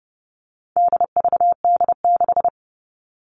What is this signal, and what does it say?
Morse code, heard as a keyed tone, about 30 wpm, 700 Hz. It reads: DVD6
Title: All Cut Numbers - 4-Digit 30wpm